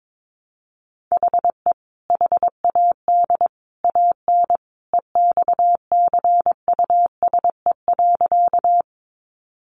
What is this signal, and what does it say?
Morse code, heard as a keyed tone, about 22 wpm, 700 Hz. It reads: HE HAD AN EXCUSE.
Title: Sentences from Top 800 Words 22wpm